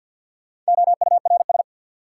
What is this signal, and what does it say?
Morse code, heard as a keyed tone, about 47 wpm, 700 Hz. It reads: OURS
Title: Sentences from Top 1000 Words - Review Part 5 - New Word Drill - 50wpm